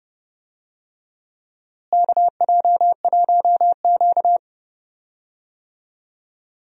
Morse code, heard as a keyed tone, about 30 wpm, 700 Hz. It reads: KJ1Q
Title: USA Call Signs - 4 Characters - Repeated 2x - 30wpm